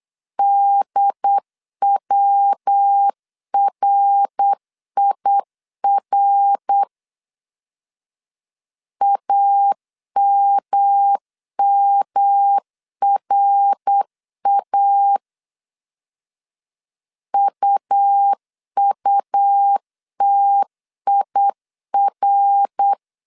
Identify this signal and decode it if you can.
Morse code, heard as a keyed tone, about 8 wpm, 800 Hz. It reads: DWRIR AMMRA UUTIR
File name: CW / DET / 1 3 1500C S B 8wpm 4141